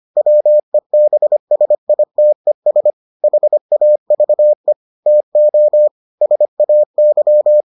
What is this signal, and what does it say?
Morse code, heard as a keyed tone, about 25 wpm, 600 Hz. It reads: WEBSITES HAVE TO SAY